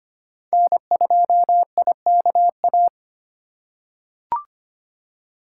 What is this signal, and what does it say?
Morse code, heard as a keyed tone, about 25 wpm, 700 Hz. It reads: N2IKA
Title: USA Call Signs - 5 Characters - 2x Character Spacing - 25wpm